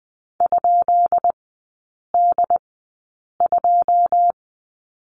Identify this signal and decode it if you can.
Morse code, heard as a keyed tone, about 20 wpm, spaced as for 8 wpm, 700 Hz. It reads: ?D2
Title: 133 Lesson 22e - Question Mark - Three Characters 20wpm